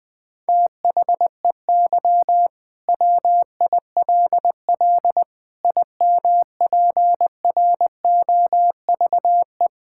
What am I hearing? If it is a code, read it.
Morse code, heard as a keyed tone, about 21 wpm, 700 Hz. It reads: THEY WILL IMPROVE
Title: Sentences from Top 1000 Words - Review Part 5 - 20wpm